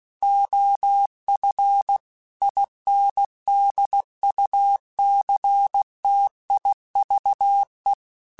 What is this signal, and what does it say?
Morse code, heard as a keyed tone, about 16 wpm, 800 Hz. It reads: OF INDUCTIVE
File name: Technical Subjects / Technical Subjects-mp3s / XL @15